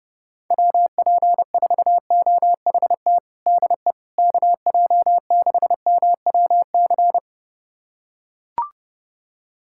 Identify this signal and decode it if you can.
Morse code, heard as a keyed tone, about 32 wpm, 700 Hz. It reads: WP4OHT DE KJ6MWC
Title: USA Callsign DE Callsign 30wpm